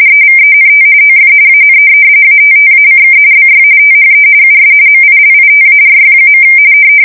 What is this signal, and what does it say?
RTTY, about 45.45 baud. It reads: EVENT OF AN ATTACK BY THE US MILITARY.
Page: 170 Shift 2125/2295